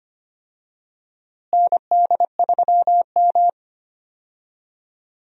Morse code, heard as a keyed tone, about 25 wpm, 700 Hz. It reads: ND3M
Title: USA Call Signs - 4 Characters - Repeated 3x - 25wpm